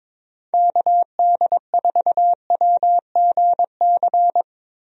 Morse code, heard as a keyed tone, about 22 wpm, 700 Hz. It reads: KD4WGC
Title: USA Call Signs - 6 Characters - Repeated 3x - 22wpm